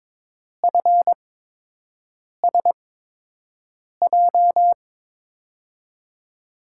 Morse code, heard as a keyed tone, about 21 wpm, spaced as for 6 wpm, 700 Hz. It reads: FSJ
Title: USA Call Signs - 8x Character Spacing - 22wpm